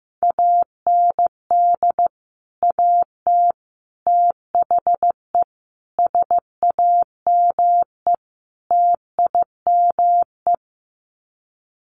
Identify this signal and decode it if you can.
Morse code, heard as a keyed tone, about 15 wpm, 700 Hz. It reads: AND AT THE SAME TIME
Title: Sets of 5 Words 15wpm